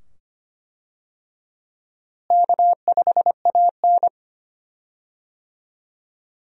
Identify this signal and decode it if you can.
Morse code, heard as a keyed tone, about 25 wpm, 700 Hz. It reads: K5AN
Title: USA Call Signs - 4 Characters - Repeated 2x - 25wpm